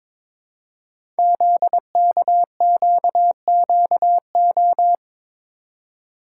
Morse code, heard as a keyed tone, about 22 wpm, 690 Hz. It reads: ZKQQO
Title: Code Groups - 5 Letters - 22wpm